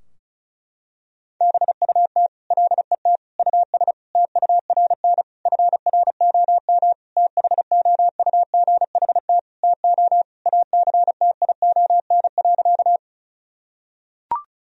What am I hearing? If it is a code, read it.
Morse code, heard as a keyed tone, about 34 wpm, 700 Hz. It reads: BUT LET US TURN FROM THOUGHT TO ACTION.
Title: Sentences from Top 1000 Words - Review Part 4 - 35wpm